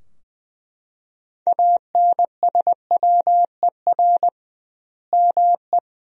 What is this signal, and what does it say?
Morse code, heard as a keyed tone, about 20 wpm, 700 Hz. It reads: ANSWER ME
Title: 2x Word-Spacing - Sentences from Top 400 Words - 20wpm